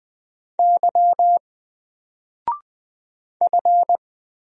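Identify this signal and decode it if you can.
Morse code, heard as a keyed tone, about 20 wpm, 700 Hz. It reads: Y F
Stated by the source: Single letter and number Morse drill at 20 wpm, from N4LCD.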